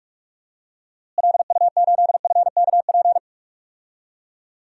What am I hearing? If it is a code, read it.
Morse code, heard as a keyed tone, about 45 wpm, 700 Hz. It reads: LU8FKP
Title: DXCC-100 Call Signs - 6 Characters - Repeated 3x - 45wpm